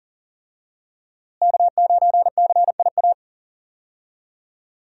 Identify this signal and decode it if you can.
Morse code, heard as a keyed tone, about 38 wpm, 700 Hz. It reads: K9CIA
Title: USA Call Signs - 5 Characters - Repeated 3x - 40wpm